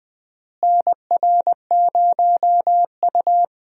Morse code, heard as a keyed tone, about 20 wpm, 700 Hz. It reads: NR0U
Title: USA Call Signs - 4 Characters - Repeated 3x - 20wpm